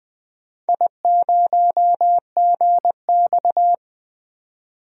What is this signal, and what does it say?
Morse code, heard as a keyed tone, about 20 wpm, 700 Hz. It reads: I0GX
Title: DXCC-100 Call Signs - 4 Characters - 20wpm